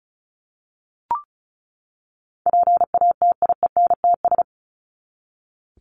Morse code, heard as a keyed tone, about 34 wpm, 700 Hz. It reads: PATIENTS
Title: ES:S Suffix - Top 100 Words with Base Words 35wpm